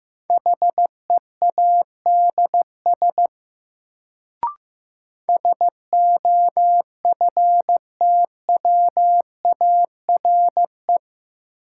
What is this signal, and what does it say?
Morse code, heard as a keyed tone, about 16 wpm, 700 Hz. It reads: HEADS SOFTWARE
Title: Sentences from Top 1000 Words - Review Part 3 - New Word Drill - 15wpm